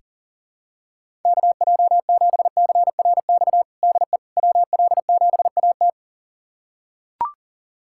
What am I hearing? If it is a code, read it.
Morse code, heard as a keyed tone, about 37 wpm, 700 Hz. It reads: KJ7CRX DE WL7AT
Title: USA Callsign DE Callsign 40wpm